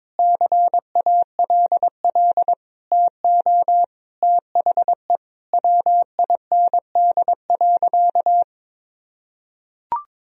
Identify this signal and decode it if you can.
Morse code, heard as a keyed tone, about 22 wpm, 700 Hz. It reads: CALL TO THE WIND.
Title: Sentences from Top 1000 Words - Review Part 2 - 22wpm